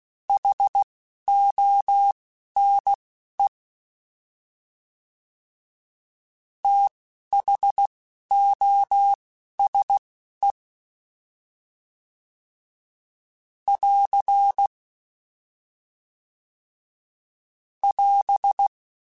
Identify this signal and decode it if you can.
Morse code, heard as a keyed tone, about 16 wpm, spaced as for 12 wpm, 800 Hz. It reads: HONE THOSE <AR> <AS>
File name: T then H Runs / T then H Runs.mp3s / T then H Runs @10